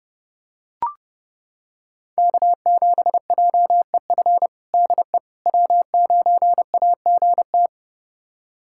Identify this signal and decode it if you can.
Morse code, heard as a keyed tone, about 32 wpm, 700 Hz. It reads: K7JEF DE W9AGT
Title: USA Callsign DE Callsign - 5 Characters - 30wpm